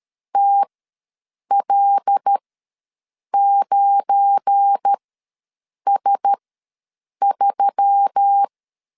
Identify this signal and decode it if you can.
Morse code, heard as a keyed tone, about 13 wpm, spaced as for 7 wpm, 800 Hz. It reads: TL9S3
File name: LEZIONI / 7 GF9Z / 7 GF9 ws 12wpm 3344